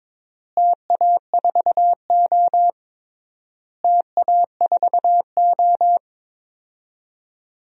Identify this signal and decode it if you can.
Morse code, heard as a keyed tone, about 22 wpm, 700 Hz. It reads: TA4O TA4O
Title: DXCC-100 Call Signs - 4 Characters - Repeated 3x - 22wpm